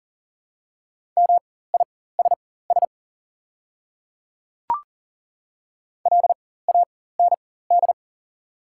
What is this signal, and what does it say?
Morse code, heard as a keyed tone, about 40 wpm, spaced as for 19 wpm, 700 Hz. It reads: MISS LAND
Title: 4x Character Spacing - Top 500 Words - 40wpm